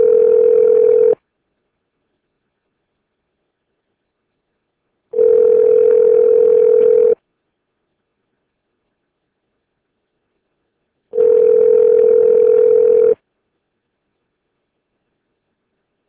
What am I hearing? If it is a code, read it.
Ringback tone (North America), 440 + 480 Hz.